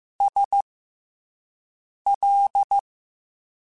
Morse code, heard as a keyed tone, about 15 wpm, spaced as for 5 wpm, 800 Hz. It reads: SL